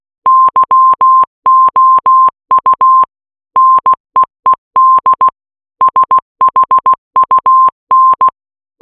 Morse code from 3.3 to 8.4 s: NEED SHUN